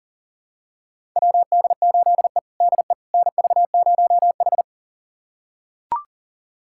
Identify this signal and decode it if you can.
Morse code, heard as a keyed tone, about 40 wpm, 700 Hz. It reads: WD8E DE NV0H
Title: USA Callsign DE Callsign - 4 Characters - 40wpm